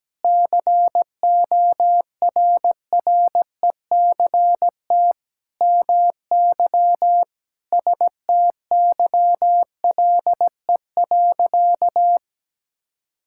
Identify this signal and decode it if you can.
Morse code, heard as a keyed tone, about 17 wpm, 700 Hz. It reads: CORRECT MY STYLE.
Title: Sentences from Top 1000 Words - Review Part 5 - 17wpm